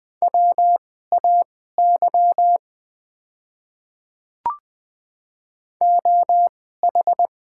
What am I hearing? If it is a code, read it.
Morse code, heard as a keyed tone, about 20 wpm, spaced as for 15 wpm, 690 Hz. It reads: WAY OH
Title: 2x Character Spacing - Top 500 Words - 20wpm